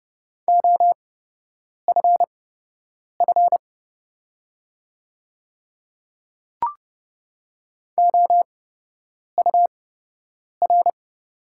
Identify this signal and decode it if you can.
Morse code, heard as a keyed tone, about 31 wpm, spaced as for 8 wpm, 690 Hz. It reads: OFF OUR
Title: 8x Character Spacing - Top 200 Words - 30wpm